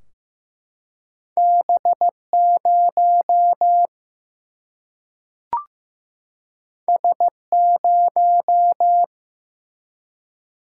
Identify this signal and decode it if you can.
Morse code, heard as a keyed tone, about 15 wpm, 700 Hz. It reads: B 0 S 0
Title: Single-Letter Single-Number 15wpm